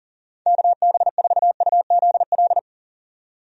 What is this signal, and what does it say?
Morse code, heard as a keyed tone, about 38 wpm, 700 Hz. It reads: KB4UZL
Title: USA Call Signs - 6 Characters - 40wpm